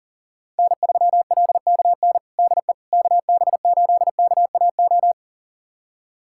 Morse code, heard as a keyed tone, about 37 wpm, 700 Hz. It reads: N3LKN DE KB8KAO
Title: USA Callsign DE Callsign 40wpm